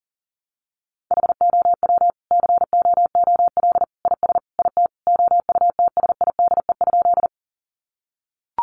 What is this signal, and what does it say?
Morse code, heard as a keyed tone, about 37 wpm, 700 Hz. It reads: HOW COOL IS IT OUTSIDE?
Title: Sentences from Top 1200 Words 40wpm